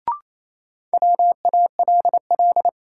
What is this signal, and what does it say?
Morse code, heard as a keyed tone, about 28 wpm, 700 Hz. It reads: WALL
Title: Sentences from Top 1000 Words - Review Part 6 - New Word Drill - 28wpm